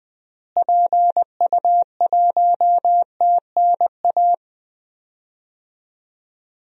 Morse code, heard as a keyed tone, about 20 wpm, 700 Hz. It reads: PU1TNA
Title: DXCC-100 Call Signs - 6 Characters - Repeated 2x - 20wpm